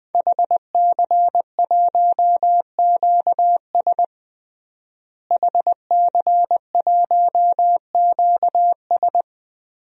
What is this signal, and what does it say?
Morse code, heard as a keyed tone, about 21 wpm, 700 Hz. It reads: HC1QS HC1QS
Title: DXCC-100 Call Signs - 5 Characters - Repeated 3x - 20wpm